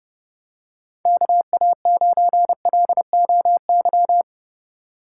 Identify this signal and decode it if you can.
Morse code, heard as a keyed tone, about 32 wpm, 700 Hz. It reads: KA9LOY
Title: USA Call Signs - 6 Characters - 8x Character Spacing - 30wpm